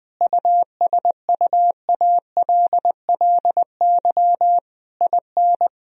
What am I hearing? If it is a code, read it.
Morse code, heard as a keyed tone, about 21 wpm, 700 Hz. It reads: USUALLY IN
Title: Sentences from Top 1000 Words - Review Part 3 - 20wpm